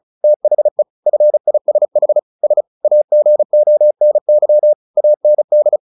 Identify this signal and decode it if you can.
Morse code, heard as a keyed tone, about 34 wpm, 600 Hz. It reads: THE FISH S AGONY AND
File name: the old man and the sea 2 35wpm